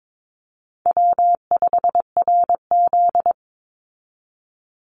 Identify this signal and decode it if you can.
Morse code, heard as a keyed tone, about 22 wpm, 700 Hz. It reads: W5RZ
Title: Call Signs Repeated 3x - 22wpm